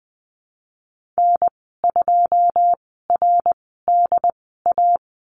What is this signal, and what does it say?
Morse code, heard as a keyed tone, about 20 wpm, spaced as for 14 wpm, 700 Hz. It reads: N2RDA
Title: 088 Lesson 15g - Number 5 - Call Signs 20wpm